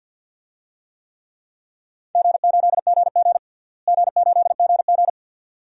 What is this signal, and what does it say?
Morse code, heard as a keyed tone, about 51 wpm, 700 Hz. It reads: G8GG G8GG
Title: DXCC-100 Call Signs - 4 Characters - Repeated 2x - 50wpm